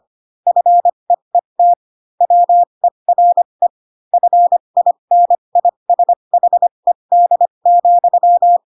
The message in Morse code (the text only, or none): FEET WERE FINISHED,